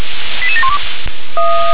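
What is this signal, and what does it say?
Five-tone selective call (ZVEI), heard as quick runs of five tones, tones of about 70 ms. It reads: 88712
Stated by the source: Received on an SDR, ZVEI five-tone selective call.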